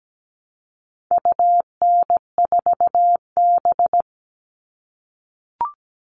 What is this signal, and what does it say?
Morse code, heard as a keyed tone, about 17 wpm, 700 Hz. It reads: UN4B
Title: All Cut Numbers - 4-Digit 17wpm